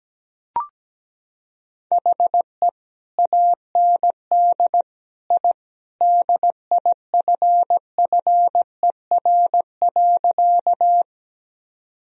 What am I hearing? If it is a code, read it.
Morse code, heard as a keyed tone, about 17 wpm, 700 Hz. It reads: HE AND I DIFFER.
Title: Sentences from Top 1000 Words - Review Part 1 - 17wpm